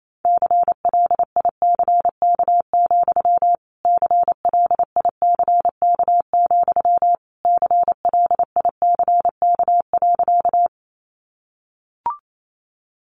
Morse code, heard as a keyed tone, about 28 wpm, 700 Hz. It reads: CLICK, CLICK, CLICK.
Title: Sentences from Top 1100 Words 28wpm